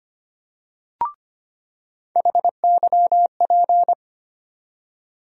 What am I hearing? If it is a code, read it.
Morse code, heard as a keyed tone, about 25 wpm, 700 Hz. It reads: HYP
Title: USA Call Sign Postfixes - 3 Letters 25wpm